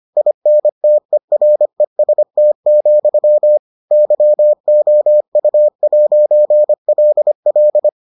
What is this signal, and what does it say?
Morse code, heard as a keyed tone, about 25 wpm, 600 Hz. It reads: INTEREST, YOU'LL